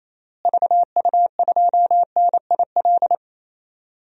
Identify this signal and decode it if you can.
Morse code, heard as a keyed tone, about 28 wpm, 700 Hz. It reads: VU2NIL
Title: DXCC-100 Call Signs - 6 Characters - Repeated 2x - 28wpm